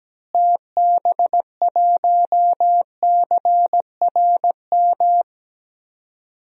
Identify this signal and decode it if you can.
Morse code, heard as a keyed tone, about 17 wpm, 700 Hz. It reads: TB1CRM
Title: DXCC-100 Call Signs - 6 Characters - Repeated 3x - 17wpm